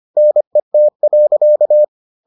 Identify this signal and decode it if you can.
Morse code, heard as a keyed tone, about 25 wpm, 600 Hz. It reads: NET.